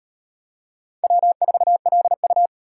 Morse code, heard as a keyed tone, about 38 wpm, 700 Hz. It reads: W4LU